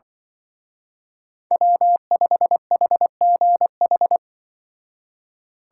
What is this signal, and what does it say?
Morse code, heard as a keyed tone, about 24 wpm, 700 Hz. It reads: W5HGH
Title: Speed Racing - USA 1x3 Call Signs 20wpm